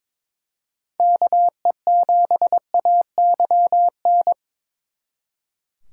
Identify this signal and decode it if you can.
Morse code, heard as a keyed tone, about 26 wpm, 700 Hz. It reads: KE7AYN
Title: Speed Racing - USA 2x3 Call Signs 22wpm